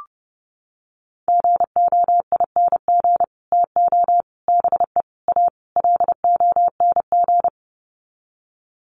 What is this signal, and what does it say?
Morse code, heard as a keyed tone, about 30 wpm, 700 Hz. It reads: GOING TO BE A LONG
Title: Sets of 5 Words Encore2 30wpm